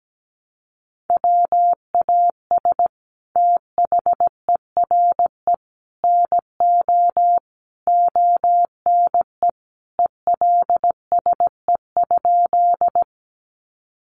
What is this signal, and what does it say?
Morse code, heard as a keyed tone, about 17 wpm, 700 Hz. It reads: WAS THERE NO ONE ELSE?
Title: Sentences from Top 800 Words 17wpm